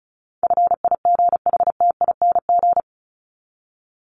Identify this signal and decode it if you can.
Morse code, heard as a keyed tone, about 34 wpm, 700 Hz. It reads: FIGHTING
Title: ING Suffix - Top 100 Words with Base Words 35wpm